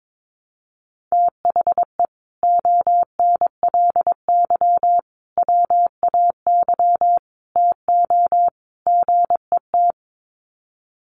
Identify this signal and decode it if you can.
Morse code, heard as a keyed tone, about 22 wpm, 700 Hz. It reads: THE ONLY WAY TO GET
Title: Sets of 5 Words Encore2 22wpm